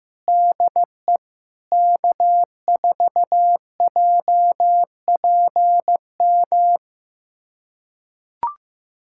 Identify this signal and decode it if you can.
Morse code, heard as a keyed tone, about 15 wpm, 700 Hz. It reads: DE K4JPM
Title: USA Callsign DE Callsign - 5 Characters - 15wpm